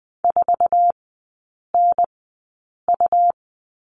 Morse code, heard as a keyed tone, about 20 wpm, spaced as for 8 wpm, 700 Hz. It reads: 4NU
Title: 100 Lesson 17e - Letter u - Three Characters 20wpm